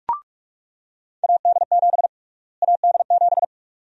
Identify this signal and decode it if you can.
Morse code, heard as a keyed tone, about 46 wpm, 700 Hz. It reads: AD7 AD7
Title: USA Call Sign Prefixes - Repeated 2x - 45wpm